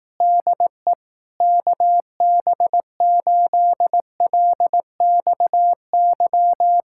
Morse code, heard as a keyed tone, about 18 wpm, 700 Hz. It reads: DE KB8LXY
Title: Speed-Racing Common QSO Phrases Encore - 15wpm